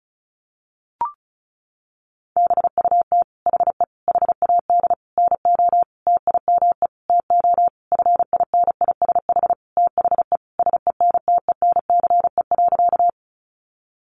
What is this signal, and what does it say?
Morse code, heard as a keyed tone, about 34 wpm, 700 Hz. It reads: BUT HE HAD NO TIME TO FINISH THE SENTENCE.
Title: Sentences from Top 1200 Words 35wpm